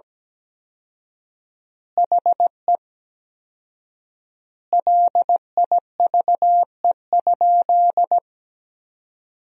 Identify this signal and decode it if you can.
Morse code, heard as a keyed tone, about 17 wpm, 700 Hz. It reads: HE LIVE?
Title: 4x Word-Spacing - Sentences from Top 200 Words - 17wpm